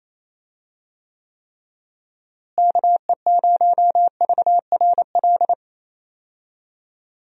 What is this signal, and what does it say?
Morse code, heard as a keyed tone, about 28 wpm, 700 Hz. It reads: KE0VRL
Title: USA Call Signs - 6 Characters - Repeated 3x - 28wpm